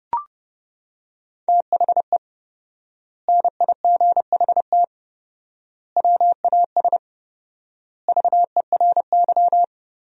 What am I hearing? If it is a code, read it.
Morse code, heard as a keyed tone, about 32 wpm, 700 Hz. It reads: THE NIGHT WAS VERY
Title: 4x Word-Spacing - Sentences from Top 500 Words - 30wpm